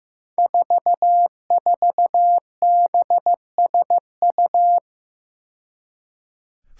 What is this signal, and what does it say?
Morse code, heard as a keyed tone, about 15 wpm, 700 Hz. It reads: 44BSU
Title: Five Challenging Letter-Number Combinations 15wpm